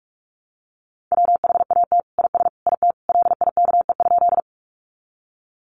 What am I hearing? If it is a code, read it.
Morse code, heard as a keyed tone, about 44 wpm, 700 Hz. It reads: WHAT IS IT LIKE?
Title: Sentences from Top 100 Words 45wpm